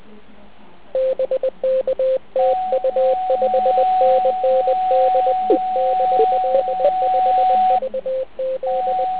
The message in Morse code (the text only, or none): BK TU 5NN DE DL5VZ